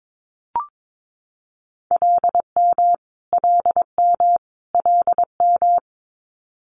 Morse code, heard as a keyed tone, about 22 wpm, 700 Hz. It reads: LM LM LM
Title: Multi-letter Phonemes Repeated 3x 22wpm